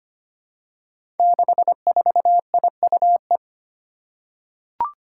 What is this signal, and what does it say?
Morse code, heard as a keyed tone, about 25 wpm, 700 Hz. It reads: 64IUE
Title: Five Challenging Letter-Number Combinations 25wpm